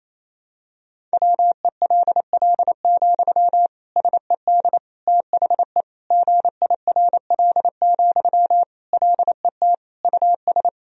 Morse code, heard as a keyed tone, about 27 wpm, 700 Hz. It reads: WELL, SED THE GIRL, LET US
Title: Sentences from Top 400 Words and Abbreviations 28wpm